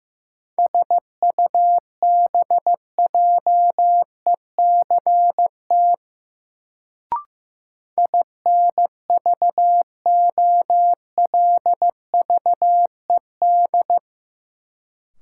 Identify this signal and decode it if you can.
Morse code, heard as a keyed tone, about 15 wpm, 700 Hz. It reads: SUBJECT INVOLVED
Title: Sentences from Top 800 Words - New Word Drill - 15wpm